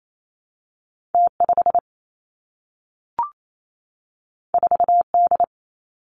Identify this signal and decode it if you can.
Morse code, heard as a keyed tone, about 28 wpm, 700 Hz. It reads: T 5 4 D
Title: Two Challenging Letter-Number Combinations 28wpm